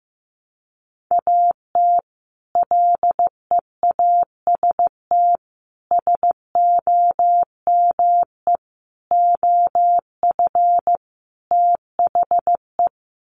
Morse code, heard as a keyed tone, about 15 wpm, 700 Hz. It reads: AT LEAST SOME OF THE
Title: Sets of 5 Words Encore 15wpm